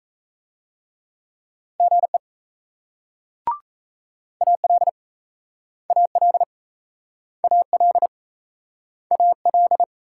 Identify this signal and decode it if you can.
Morse code, heard as a keyed tone, about 38 wpm, 700 Hz. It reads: GE AL AL AL AL